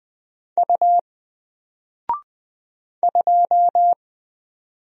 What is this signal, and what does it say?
Morse code, heard as a keyed tone, about 20 wpm, 700 Hz. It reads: U 2
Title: Single Character (Letter:Number:Punctuation:Prosign) 20wpm